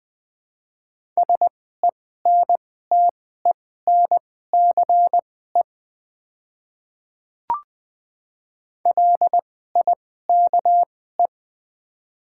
Morse code, heard as a keyed tone, about 20 wpm, spaced as for 14 wpm, 700 Hz. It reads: SENTENCE LIKE
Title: 2x Character Spacing - Top 200 Words - 20wpm